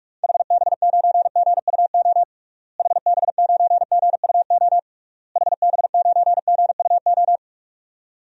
Morse code, heard as a keyed tone, about 41 wpm, 700 Hz. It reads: HB9GUO HB9GUO HB9GUO
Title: DXCC-100 Call Signs - 6 Characters - Repeated 3x - 45wpm